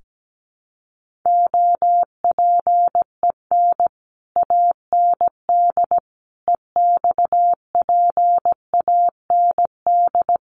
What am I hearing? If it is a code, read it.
Morse code, heard as a keyed tone, about 17 wpm, 700 Hz. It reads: OPEN AND EXPAND
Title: Sentences from Top 1400 Words 17wpm